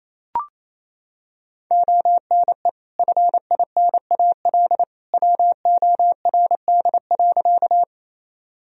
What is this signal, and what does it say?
Morse code, heard as a keyed tone, about 28 wpm, 700 Hz. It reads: ONE FINAL WORD.
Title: Sentences from Top 1000 Words - Review Part 2 - 28wpm